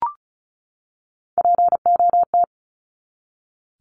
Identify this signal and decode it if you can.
Morse code, heard as a keyed tone, about 34 wpm, 700 Hz. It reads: POT
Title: Top 751-1000 Words 35wpm